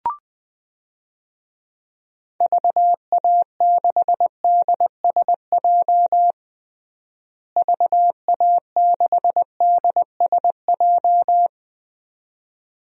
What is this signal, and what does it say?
Morse code, heard as a keyed tone, about 21 wpm, 700 Hz. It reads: VA6DSJ VA6DSJ
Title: DXCC-100 Call Signs - 6 Characters - Repeated 2x - 20wpm